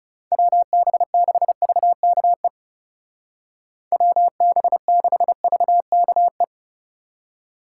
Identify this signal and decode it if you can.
Morse code, heard as a keyed tone, about 38 wpm, 700 Hz. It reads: WB6VKE WB6VKE